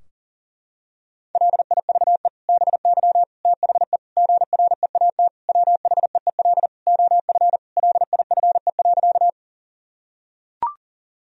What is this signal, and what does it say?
Morse code, heard as a keyed tone, about 40 wpm, 700 Hz. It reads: LIVE BY THE GREAT WHEEL OF LIFE.
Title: Sentences from Top 1000 Words - Review Part 3 - 40wpm